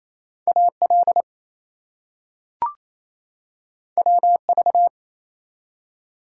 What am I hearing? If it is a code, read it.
Morse code, heard as a keyed tone, about 27 wpm, 700 Hz. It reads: AL WV
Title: US States and Canadian Provinces 28wpm